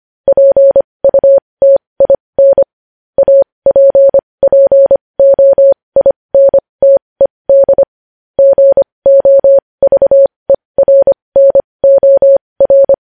Morse code, heard as a keyed tone, about 25 wpm, spaced as for 20 wpm, 550 Hz. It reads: PUTIN APPOINTED GOVERNOR
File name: podcast / News headlines-1x-25x20